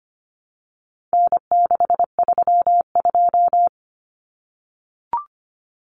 Morse code, heard as a keyed tone, about 25 wpm, 700 Hz. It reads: N632
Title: Common Cut Numbers - 4-Digit 25wpm